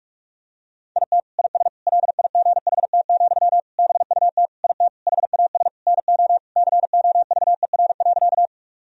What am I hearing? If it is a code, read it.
Morse code, heard as a keyed tone, about 42 wpm, 700 Hz. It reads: IT IS LIGHT, BUT IT HAS NO COVER.
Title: Sentences from Top 1000 Words - Review Part 1 - 45wpm